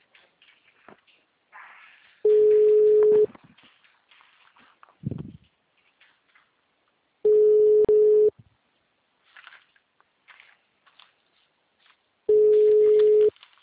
Ringback tone (Europe, 425 Hz).
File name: out- -2013- - - .69101